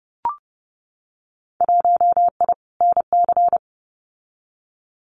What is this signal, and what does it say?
Morse code, heard as a keyed tone, about 30 wpm, 700 Hz. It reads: 1I NC
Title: Winter Field Day Exchanges Encore 30wpm